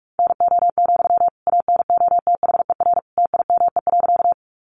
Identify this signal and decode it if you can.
Morse code, heard as a keyed tone, about 44 wpm, 700 Hz. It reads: NO, ANOTHER TIME.
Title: Sentences from Top 600 Words 45wpm